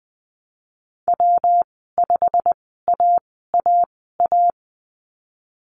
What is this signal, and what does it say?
Morse code, heard as a keyed tone, about 20 wpm, spaced as for 15 wpm, 700 Hz. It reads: W5AAA
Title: 147 Lesson 24g - Letter y - Call Signs 20wpm